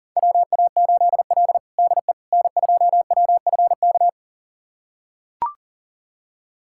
Morse code, heard as a keyed tone, about 37 wpm, 700 Hz. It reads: WA8L DE N2WFK
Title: USA Callsign DE Callsign 40wpm